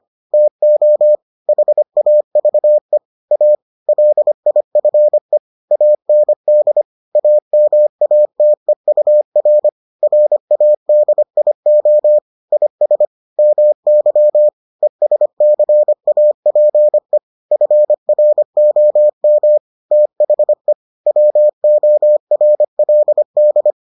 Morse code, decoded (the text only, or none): TO HAVE A LIFE AND AMATEUR RADIO IS MY ESCAPE FROM THE WORLD